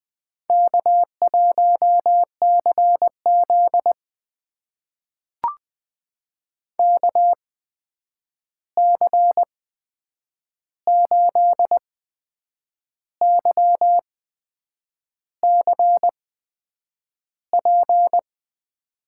Morse code, heard as a keyed tone, about 20 wpm, spaced as for 8 wpm, 700 Hz. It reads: K1CZ KC8YCP
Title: USA Call Signs - 8x Character Spacing - 20wpm